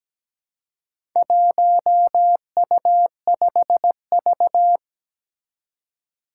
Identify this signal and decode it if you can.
Morse code, heard as a keyed tone, about 17 wpm, 700 Hz. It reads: 1U5V
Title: Four Challenging Letter-Number Combinations 17wpm